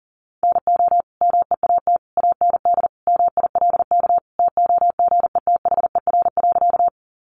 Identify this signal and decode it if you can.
Morse code, heard as a keyed tone, about 40 wpm, 700 Hz. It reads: NO MEAT AND MILK TOGETHER.
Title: Sentences from Top 1500 Words 40wpm